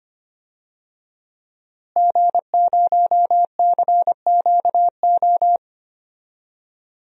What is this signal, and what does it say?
Morse code, heard as a keyed tone, about 25 wpm, 700 Hz. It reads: G0CQO
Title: DXCC-100 Call Signs - 5 Characters - Repeated 2x - 25wpm